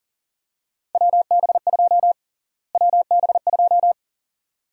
Morse code, heard as a keyed tone, about 40 wpm, 700 Hz. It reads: WB2 WB2
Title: USA Call Sign Prefixes - Repeated 2x - 40wpm